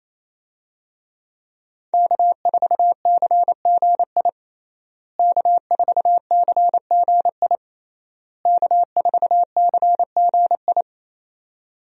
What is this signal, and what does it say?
Morse code, heard as a keyed tone, about 28 wpm, 700 Hz. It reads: K4CGI K4CGI K4CGI
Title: USA Call Signs - 5 Characters - Repeated 3x - 28wpm